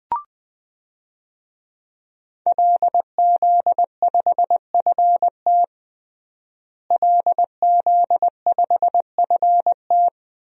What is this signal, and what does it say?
Morse code, heard as a keyed tone, about 20 wpm, 700 Hz. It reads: LZ5FT LZ5FT
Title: DXCC-100 Call Signs - 5 Characters - Repeated 2x - 20wpm